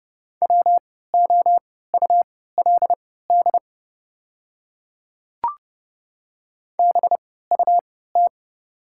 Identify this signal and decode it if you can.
Morse code, heard as a keyed tone, about 31 wpm, spaced as for 17 wpm, 690 Hz. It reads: WOULD BUT
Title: 3x Character Spacing - Top 100 Words - 30wpm